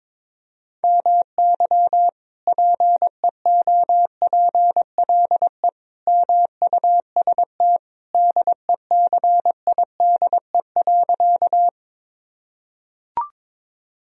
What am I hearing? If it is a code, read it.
Morse code, heard as a keyed tone, about 22 wpm, 700 Hz. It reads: MY PEOPLE MUST DECIDE.
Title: Sentences from Top 1000 Words - Review Part 3 - 22wpm